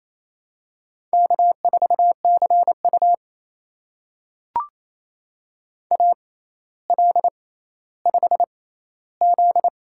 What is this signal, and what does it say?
Morse code, heard as a keyed tone, about 28 wpm, spaced as for 14 wpm, 700 Hz. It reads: K4CU AL5Z
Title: USA Call Signs - 4 Characters - 6x Character Spacing - 28wpm